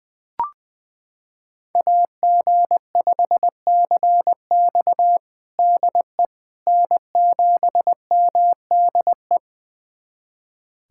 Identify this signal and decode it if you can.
Morse code, heard as a keyed tone, about 20 wpm, 700 Hz. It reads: AG5CX DE N7MDE
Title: USA Callsign DE Callsign - 5 Characters - 20wpm